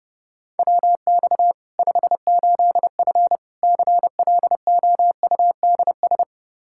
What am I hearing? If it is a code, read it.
Morse code, heard as a keyed tone, about 32 wpm, 700 Hz. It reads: WX 58F CLOUDS
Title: Practice QSO - Exchange 2 - Info - 30wpm